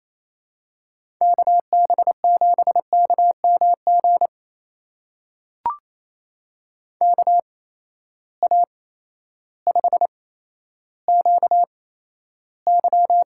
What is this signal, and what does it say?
Morse code, heard as a keyed tone, about 28 wpm, spaced as for 13 wpm, 700 Hz. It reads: KB7KMG KA5QY
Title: USA Call Signs - 6 Characters - 8x Character Spacing - 28wpm